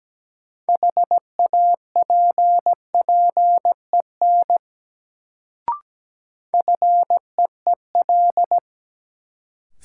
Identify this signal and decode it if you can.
Morse code, heard as a keyed tone, about 17 wpm, 700 Hz. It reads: HAPPEN FEEL
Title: Sentences from Top 400 Words - New Word Drill - 17wpm